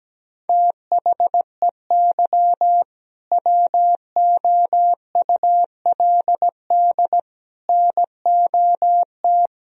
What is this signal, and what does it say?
Morse code, heard as a keyed tone, about 17 wpm, 700 Hz. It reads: THEY WOULD NOT
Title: Sentences from Top 1000 Words - Review Part 3 - 17wpm